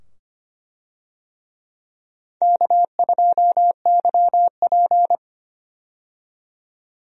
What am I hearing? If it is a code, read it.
Morse code, heard as a keyed tone, about 25 wpm, 700 Hz. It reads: K2YP
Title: USA Call Signs - 4 Characters - 25wpm